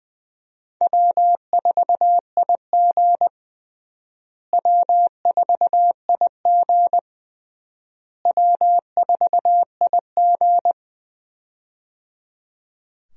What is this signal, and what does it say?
Morse code, heard as a keyed tone, about 20 wpm, 700 Hz. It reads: W4IG W4IG W4IG
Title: USA Call Signs - 4 Characters - Repeated 3x - 20wpm